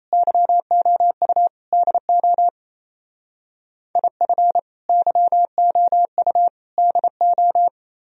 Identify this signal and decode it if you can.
Morse code, heard as a keyed tone, about 34 wpm, 700 Hz. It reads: YOU DO IF YOU DO
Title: Speed-Racing - Sets of 3 Words 28wpm